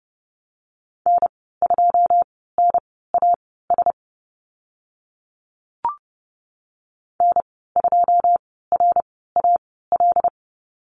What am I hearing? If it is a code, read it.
Morse code, heard as a keyed tone, about 30 wpm, spaced as for 17 wpm, 700 Hz. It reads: N2NAS N2RAL
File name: Lesson 15g - Number 5 - Call Signs 30wpm